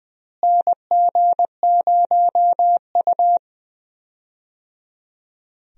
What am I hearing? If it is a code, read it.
Morse code, heard as a keyed tone, about 21 wpm, 700 Hz. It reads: NG0U
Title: USA Call Signs - 4 Characters - Repeated 3x - 20wpm